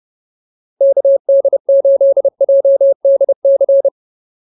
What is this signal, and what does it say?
Morse code, heard as a keyed tone, about 30 wpm, 550 Hz. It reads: KD8JDC
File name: POTA110 30